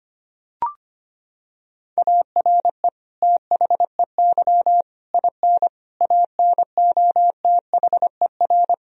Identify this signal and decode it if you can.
Morse code, heard as a keyed tone, about 25 wpm, 700 Hz. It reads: ARE THEY IN ANOTHER
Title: Sentences from Top 1000 Words - Review Part 1 - 25wpm